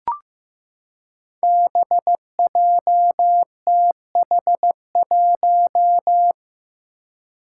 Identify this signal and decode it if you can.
Morse code, heard as a keyed tone, about 15 wpm, 700 Hz. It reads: BJTH1
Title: Five Challenging Letter-Number Combinations 15wpm